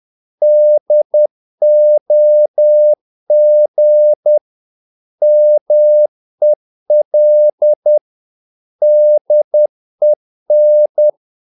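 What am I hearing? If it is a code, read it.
Morse code, heard as a keyed tone, about 10 wpm, 600 Hz. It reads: DOG MEL DEN
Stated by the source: Morse practice of short Danish words at 10 wpm, from CW med Gnister.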